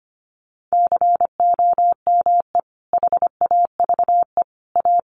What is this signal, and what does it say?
Morse code, heard as a keyed tone, about 25 wpm, 700 Hz. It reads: COME HAVE A
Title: Sentences from Top 1500 Words 25wpm